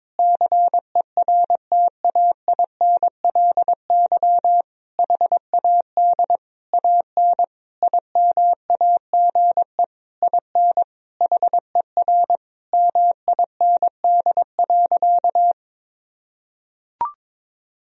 Morse code, heard as a keyed tone, about 22 wpm, 700 Hz. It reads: CERTAINLY HAD AN IMAGE IN HER MIND.
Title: Sentences from Top 1000 Words - Review Part 4 - 22wpm